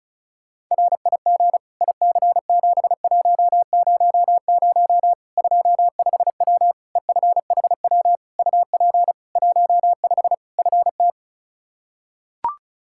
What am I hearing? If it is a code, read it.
Morse code, heard as a keyed tone, about 36 wpm, 700 Hz. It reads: RIG IC7100 25W EFHW UP 15 FT
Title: Practice QSO - Exchange 2 - Info - 35wpm